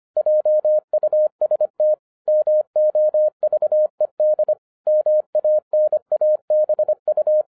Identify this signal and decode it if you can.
Morse code, heard as a keyed tone, about 26 wpm, 610 Hz. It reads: JUST MOVED MANABU
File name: 2011-10-mp3 / n04-JUST MOVED 25wpm